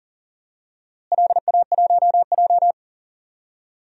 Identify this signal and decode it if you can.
Morse code, heard as a keyed tone, about 40 wpm, 700 Hz. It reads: LA1J
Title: DXCC-100 Call Signs - 4 Characters - Repeated 3x - 40wpm